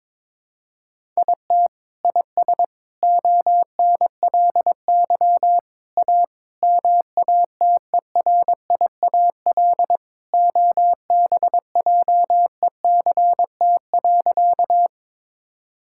Morse code, heard as a keyed tone, about 22 wpm, 700 Hz. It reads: IT IS ONLY A MATERIAL OBJECT.
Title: Sentences from Top 1000 Words - Review Part 5 - 22wpm